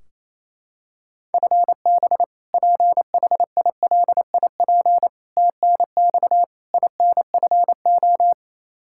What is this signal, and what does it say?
Morse code, heard as a keyed tone, about 28 wpm, 700 Hz. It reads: FB PHILIP TNX INFO
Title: Practice QSO - Exchange 3 - More Info - 28wpm